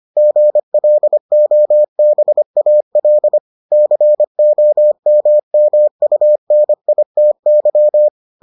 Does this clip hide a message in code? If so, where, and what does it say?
Yes: Morse code from 0.2 to 8.1 s, heard as a keyed tone, about 25 wpm, 600 Hz. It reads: GLOBAL COMMUNITY